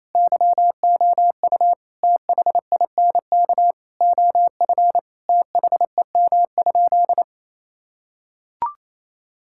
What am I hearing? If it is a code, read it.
Morse code, heard as a keyed tone, about 28 wpm, 700 Hz. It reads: YOU THINK OF THEM?
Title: Sentences from Top 1600 Words 28wpm